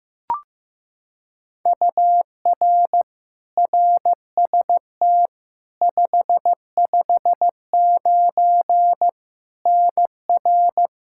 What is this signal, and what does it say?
Morse code, heard as a keyed tone, about 15 wpm, 700 Hz. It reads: UR RST 559 NR
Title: Practice QSO - Exchange 1 - Beginning - 15wpm